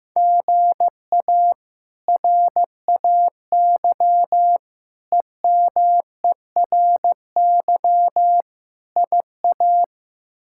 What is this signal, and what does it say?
Morse code, heard as a keyed tone, about 15 wpm, 700 Hz. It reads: GA RAY EMERY IA
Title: K1USN SST QSO Practice - 15wpm